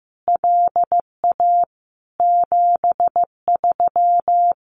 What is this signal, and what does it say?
Morse code, heard as a keyed tone, about 15 wpm, 700 Hz. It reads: LA 73
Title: Common QSO Phrases Encore 15wpm